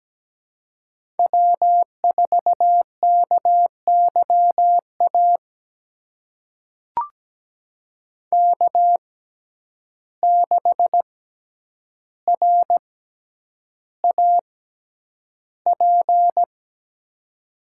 Morse code, heard as a keyed tone, about 17 wpm, spaced as for 9 wpm, 700 Hz. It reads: W4KYA K6RAP
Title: USA Call Signs - 5 Characters - 6x Character Spacing - 17wpm